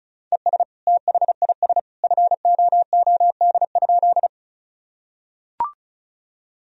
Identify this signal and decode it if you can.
Morse code, heard as a keyed tone, about 36 wpm, 700 Hz. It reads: ES THIS FOOD?
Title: Sentences from Top 300 Words and Abbreviations 35wpm